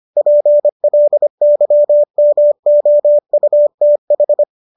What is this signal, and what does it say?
Morse code, heard as a keyed tone, about 25 wpm, 590 Hz. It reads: PLYMOUTH
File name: qso213 25